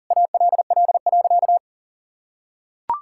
Morse code, heard as a keyed tone, about 37 wpm, 700 Hz. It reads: ALL.
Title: Sentences from Top 1000 Words - Review Part 5 - 40wpm